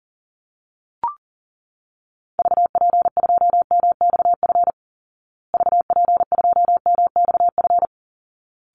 Morse code, heard as a keyed tone, about 40 wpm, 700 Hz. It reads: VP2MXF VP2MXF
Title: Call Signs Repeated 3x - 40wpm